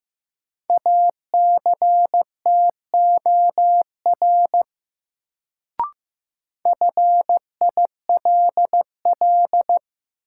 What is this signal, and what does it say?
Morse code, heard as a keyed tone, about 15 wpm, 700 Hz. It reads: ACTOR FILL
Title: Sentences from Top 1500 Words - New Word Drill Alt - 15wpm